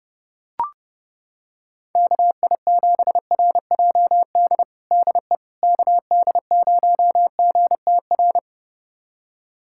Morse code, heard as a keyed tone, about 30 wpm, 700 Hz. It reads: KI7RJD DE KD0GTR
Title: USA Callsign DE Callsign - 6 Characters - 30wpm